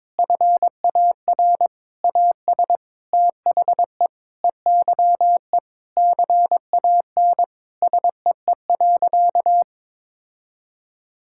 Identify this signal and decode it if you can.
Morse code, heard as a keyed tone, about 21 wpm, 700 Hz. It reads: FAR AS THE EYE CAN SEE.
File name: Sentences with 7-Word Phrases - 22wpm